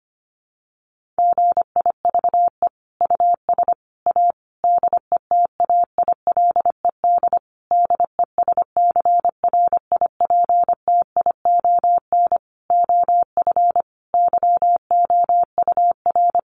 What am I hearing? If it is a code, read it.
Morse code, heard as a keyed tone, about 25 wpm, 700 Hz. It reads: GIVE US A DETAILED DESCRIPTION OF YOUR
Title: Sentences from Top 1500 Words 25wpm